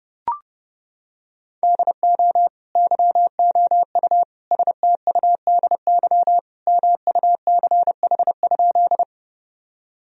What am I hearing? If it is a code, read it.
Morse code, heard as a keyed tone, about 32 wpm, 700 Hz. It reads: DO YOU STUDY MUCH?
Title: Sentences from Top 1000 Words - Review Part 1 - 30wpm